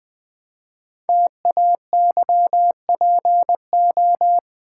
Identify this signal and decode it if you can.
Morse code, heard as a keyed tone, about 20 wpm, 700 Hz. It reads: TAYPO
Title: Code Groups - 5 Letters - 20wpm